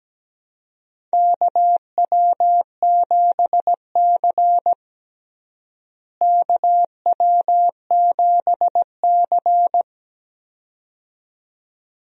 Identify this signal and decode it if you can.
Morse code, heard as a keyed tone, about 17 wpm, 700 Hz. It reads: KW7C KW7C
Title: USA Call Signs - 4 Characters - Repeated 2x - 17wpm